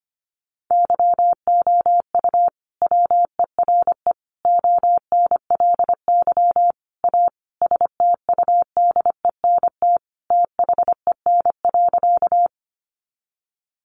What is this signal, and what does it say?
Morse code, heard as a keyed tone, about 25 wpm, 700 Hz. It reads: YOU WERE ONLY A STUDENT THEN.
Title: Sentences from Top 1000 Words 25wpm